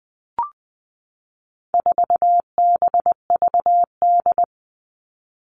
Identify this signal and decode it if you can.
Morse code, heard as a keyed tone, about 20 wpm, 700 Hz. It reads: 4BVD
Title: All Cut Numbers - 4-Digit 20wpm